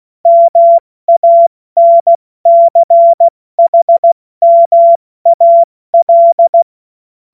Morse code, heard as a keyed tone, about 16 wpm, 680 Hz. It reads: MANCHMAL
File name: Morsekurs / Witze / Witz 01. März 2023 70 F80